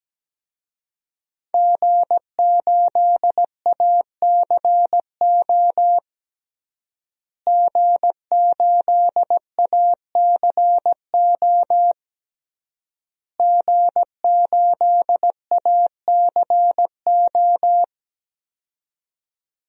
Morse code, heard as a keyed tone, about 17 wpm, 700 Hz. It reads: G8ACO G8ACO G8ACO
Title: DXCC-100 Call Signs - 5 Characters - Repeated 3x - 17wpm